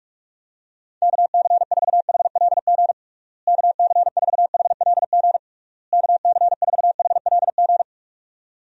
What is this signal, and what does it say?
Morse code, heard as a keyed tone, about 48 wpm, 700 Hz. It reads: KC4HLG KC4HLG KC4HLG
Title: USA Call Signs - 6 Characters - Repeated 3x - 45wpm